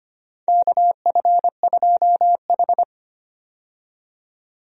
Morse code, heard as a keyed tone, about 25 wpm, 700 Hz. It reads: KF2H
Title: USA Call Signs - 4 Characters - 25wpm